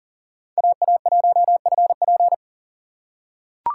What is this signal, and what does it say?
Morse code, heard as a keyed tone, about 40 wpm, 700 Hz. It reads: AA1FP
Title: USA Call Signs - 2x Character Spacing - 40wpm